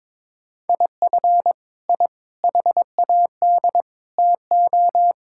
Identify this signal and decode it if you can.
Morse code, heard as a keyed tone, about 22 wpm, 700 Hz. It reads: IF I HAD TO
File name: Sentences with 9-Word Phrases - 22wpm